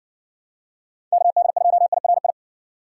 Morse code, heard as a keyed tone, about 56 wpm, 700 Hz. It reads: KD2IRI